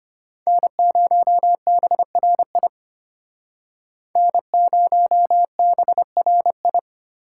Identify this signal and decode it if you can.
Morse code, heard as a keyed tone, about 33 wpm, 700 Hz. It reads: N0BRI N0BRI